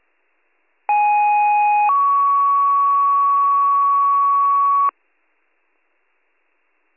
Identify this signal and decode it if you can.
Two-tone page, heard as a one-second tone, then a three-second tone, about 830 then 1120 Hz. It is heard from 0.9 to 4.9 s.